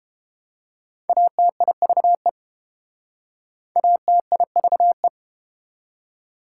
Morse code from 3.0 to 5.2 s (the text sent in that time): ATIVE